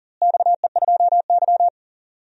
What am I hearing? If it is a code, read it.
Morse code, heard as a keyed tone, about 37 wpm, 700 Hz. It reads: XE2Y
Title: DXCC-100 Call Signs - 4 Characters - Repeated 3x - 40wpm